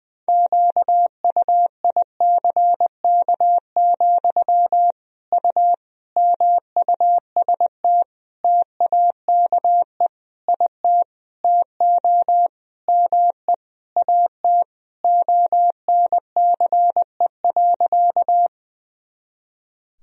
Morse code, heard as a keyed tone, about 20 wpm, 700 Hz. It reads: QUICK, U MUST TAKE IT TO ME AT ONCE.
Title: Sentences from Top 500 Words and Abbreviations 20wpm